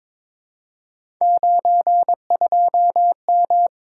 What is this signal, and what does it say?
Morse code, heard as a keyed tone, about 21 wpm, 700 Hz. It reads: 92M
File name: practice / 22wpm / Three Letter-Number Combinations - 22wpm